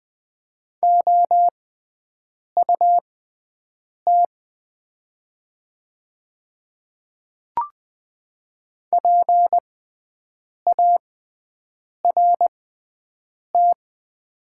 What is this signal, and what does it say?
Morse code, heard as a keyed tone, about 20 wpm, spaced as for 7 wpm, 700 Hz. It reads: OUT PART
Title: 6x Character Spacing - Top 200 Words - 20wpm